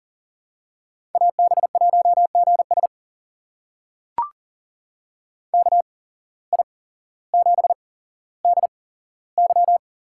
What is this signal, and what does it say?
Morse code, heard as a keyed tone, about 40 wpm, spaced as for 17 wpm, 700 Hz. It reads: AB1GS KI7DY
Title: USA Call Signs - 8x Character Spacing - 40wpm